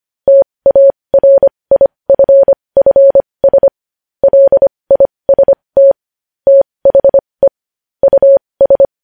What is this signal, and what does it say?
Morse code, heard as a keyed tone, about 25 wpm, spaced as for 20 wpm, 550 Hz. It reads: TARIFFS LIST THE US
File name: podcast / News headlines-1x-25x20